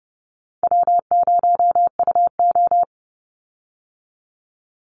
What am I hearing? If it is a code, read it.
Morse code, heard as a keyed tone, about 30 wpm, 700 Hz. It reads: W0UO
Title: Call Signs Repeated 2x - 30wpm